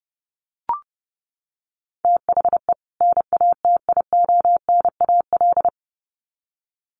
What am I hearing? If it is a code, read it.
Morse code, heard as a keyed tone, about 30 wpm, 700 Hz. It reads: THE NATIONAL
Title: Sets of 2 Words 30wpm